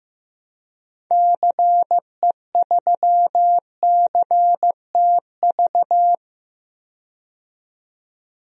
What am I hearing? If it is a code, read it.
Morse code, heard as a keyed tone, about 15 wpm, 700 Hz. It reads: CE3CTV
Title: DXCC-100 Call Signs - 6 Characters - 15wpm